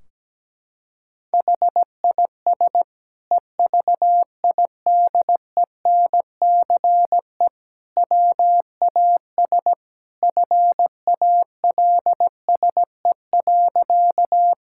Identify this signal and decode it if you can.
Morse code, heard as a keyed tone, about 17 wpm, 700 Hz. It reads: HIS EVIDENCE WAS FALSE.
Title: Sentences from Top 1600 Words 17wpm